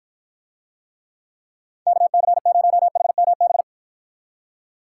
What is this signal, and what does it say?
Morse code, heard as a keyed tone, about 52 wpm, 700 Hz. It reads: KC0HMB